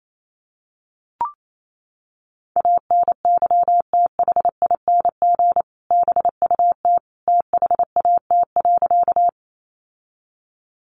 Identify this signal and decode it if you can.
Morse code, heard as a keyed tone, about 28 wpm, 700 Hz. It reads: ANYTHING BUT THAT.
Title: Sentences from Top 1000 Words 28wpm